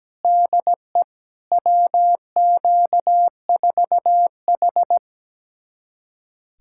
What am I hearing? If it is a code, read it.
Morse code, heard as a keyed tone, about 17 wpm, 700 Hz. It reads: DE WQ4H
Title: USA Callsign DE Callsign - 4 Characters - 17wpm